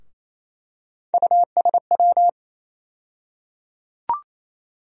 Morse code, heard as a keyed tone, about 28 wpm, 700 Hz. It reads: USW
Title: USA Call Sign Postfixes - 3 Letters 28wpm